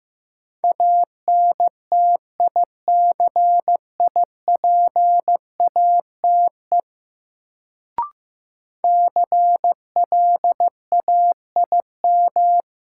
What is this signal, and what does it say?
Morse code, heard as a keyed tone, about 15 wpm, 700 Hz. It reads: ANTICIPATE CLAIM
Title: Sentences from Top 1500 Words - New Word Drill Alt - 15wpm